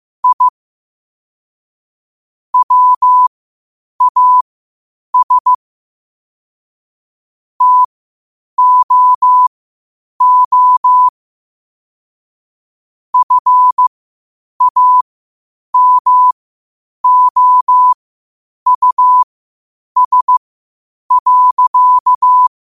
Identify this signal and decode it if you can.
Morse code, heard as a keyed tone, about 15 wpm, spaced as for 8 wpm, 1000 Hz. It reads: I WAS TOO FAMOUS.
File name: qotdfiles / 8wpm